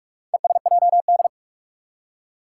Morse code, heard as a keyed tone, about 44 wpm, 700 Hz. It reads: ESJD
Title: Four Challenging Letter-Number Combinations 45wpm